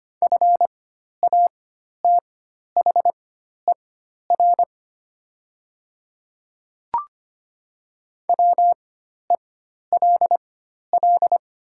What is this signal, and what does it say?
Morse code, heard as a keyed tone, about 25 wpm, spaced as for 12 wpm, 700 Hz. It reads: FATHER WELL
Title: 4x Character Spacing - Top 500 Words - 25wpm